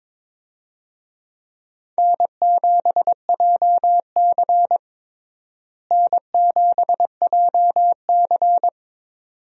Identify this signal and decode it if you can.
Morse code, heard as a keyed tone, about 22 wpm, 700 Hz. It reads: N7JC N7JC
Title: USA Call Signs - 4 Characters - Repeated 2x - 22wpm